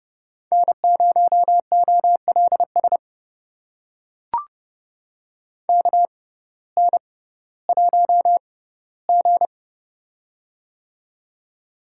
Morse code, heard as a keyed tone, about 32 wpm, spaced as for 17 wpm, 700 Hz. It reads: N0OLS KN1G
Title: USA Call Signs - 6x Character Spacing - 30wpm